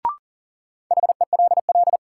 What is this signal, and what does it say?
Morse code, heard as a keyed tone, about 40 wpm, 700 Hz. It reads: HELL
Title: Sets of 5 Words Encore - New Word Drill - 40wpm